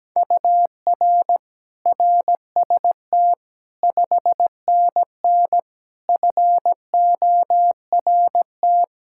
Morse code, heard as a keyed tone, about 17 wpm, 700 Hz. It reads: UR RST 5NN FORT
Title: Practice QSO - Exchange 1 - Beginning - 17wpm